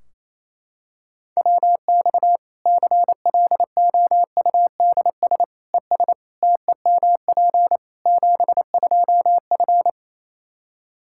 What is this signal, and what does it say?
Morse code, heard as a keyed tone, about 28 wpm, 700 Hz. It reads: WX CLOUDS ES TEMP 72F
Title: Practice QSO - Exchange 2 - Info - 28wpm